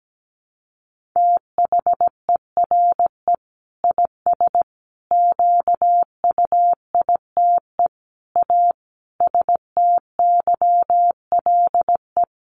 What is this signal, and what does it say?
Morse code, heard as a keyed tone, about 17 wpm, 700 Hz. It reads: THERE IS QUITE A STYLE
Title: Sentences from Top 800 Words 17wpm